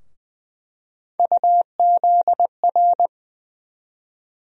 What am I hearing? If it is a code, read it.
Morse code, heard as a keyed tone, about 20 wpm, 700 Hz. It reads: UZR
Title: USA Call Sign Postfixes - 3 Letters 20wpm